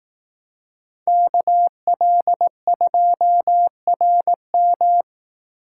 Morse code, heard as a keyed tone, about 19 wpm, 700 Hz. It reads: KL2RM
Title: Speed Racing - USA 2x2 Call Signs 15wpm